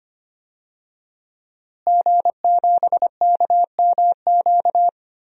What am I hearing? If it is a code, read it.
Morse code, heard as a keyed tone, about 25 wpm, 700 Hz. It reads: G7KMQ
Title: DXCC-100 Call Signs - 5 Characters - Repeated 2x - 25wpm